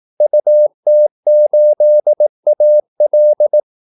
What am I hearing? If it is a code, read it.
Morse code, heard as a keyed tone, about 18 wpm, 600 Hz. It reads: UT8AL
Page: Kaldesignaler 18 wpm